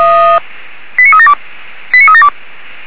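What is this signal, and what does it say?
Five-tone selective call (ZVEI), heard as quick runs of five tones, tones of about 70 ms. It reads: 89372 89372
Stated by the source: ZVEI five-tone selective call received on an SDR.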